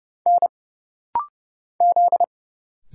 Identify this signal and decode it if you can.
Morse code, heard as a keyed tone, about 30 wpm, 700 Hz. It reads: N Z
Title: Single Letter-Number Alt - Rapid-Fire 30wpm